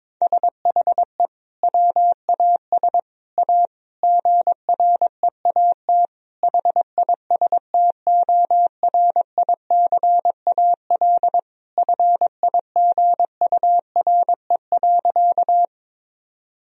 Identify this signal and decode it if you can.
Morse code, heard as a keyed tone, about 22 wpm, 700 Hz. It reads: SHE WAS A GREAT HISTORICAL FIGURE.
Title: Sentences from Top 1000 Words - Review Part 4 - 22wpm